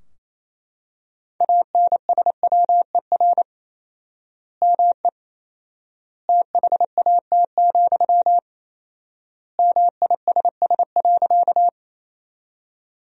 Morse code, heard as a keyed tone, about 28 wpm, 700 Hz. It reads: ANSWER ME THAT, MISS.
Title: 4x Word-Spacing - Sentences from Top 500 Words - 28wpm